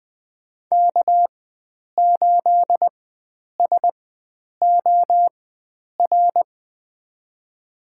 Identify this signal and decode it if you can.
Morse code, heard as a keyed tone, about 20 wpm, spaced as for 9 wpm, 700 Hz. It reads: K8SOR
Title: USA Call Signs - 5 Characters - 4x Character Spacing - 20wpm